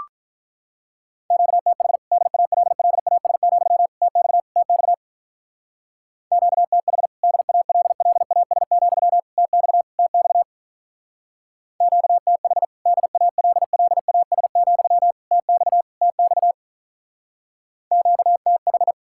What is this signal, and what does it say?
Morse code, heard as a keyed tone, about 43 wpm, 700 Hz. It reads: QTH DALLAS, TX TX QTH DALLAS, TX TX QTH DALLAS, TX TX QTH
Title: Speed-Racing Common QSO Phrases Encore - 35wpm